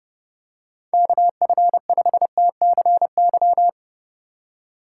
Morse code, heard as a keyed tone, about 30 wpm, 700 Hz. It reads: KF5TCY